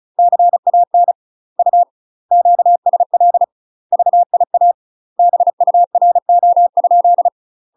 Morse code, heard as a keyed tone, about 36 wpm, 700 Hz. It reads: CAN U QSL VIA BURO?
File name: UK MASTER QSOs TRAINING / 35WPM / UK MASTER QSOs 0031